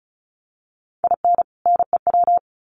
Morse code, heard as a keyed tone, about 34 wpm, 700 Hz. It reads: IN NEW
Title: Sets of 2 Words 35wpm